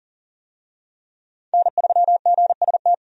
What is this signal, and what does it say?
Morse code, heard as a keyed tone, about 40 wpm, 700 Hz. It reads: N3GST
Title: USA Call Signs - 5 Characters - 40wpm